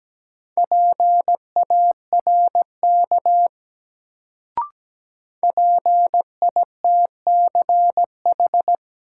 Morse code, heard as a keyed tone, about 17 wpm, 700 Hz. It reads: PARK PITCH
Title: Sentences from Top 1300 Words - New Word Drill - 17wpm